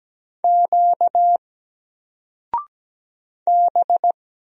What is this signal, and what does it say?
Morse code, heard as a keyed tone, about 17 wpm, 700 Hz. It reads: Q B
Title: Single Character (Letter:Number:Punctuation:Prosign) 17wpm